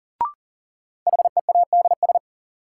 Morse code, heard as a keyed tone, about 40 wpm, 700 Hz. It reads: HEADS
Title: Sentences from Top 1000 Words - Review Part 3 - New Word Drill - 40wpm